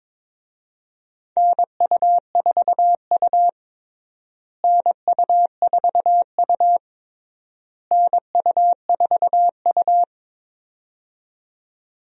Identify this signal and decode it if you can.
Morse code, heard as a keyed tone, about 22 wpm, 700 Hz. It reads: NU4U NU4U NU4U
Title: USA Call Signs - 4 Characters - Repeated 3x - 22wpm